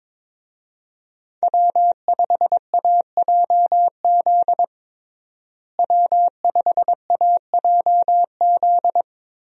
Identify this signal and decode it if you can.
Morse code, heard as a keyed tone, about 22 wpm, 700 Hz. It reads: W5AJZ W5AJZ
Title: USA Call Signs - 5 Characters - Repeated 3x - 22wpm